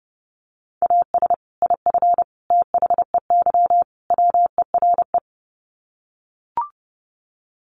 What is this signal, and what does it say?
Morse code, heard as a keyed tone, about 30 wpm, 700 Hz. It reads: AS IF THEY WERE
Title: Sets of 4 Words 30wpm